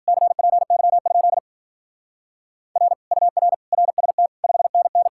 Morse code, heard as a keyed tone, about 47 wpm, 700 Hz. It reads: CPY? R UR RST 5NN
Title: Speed-Racing Common QSO Phrases Encore - 40wpm